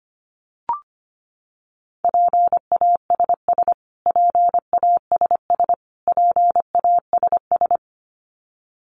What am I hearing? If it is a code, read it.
Morse code, heard as a keyed tone, about 25 wpm, 700 Hz. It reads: PASS PASS PASS
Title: Top 500 Words Repeated 25wpm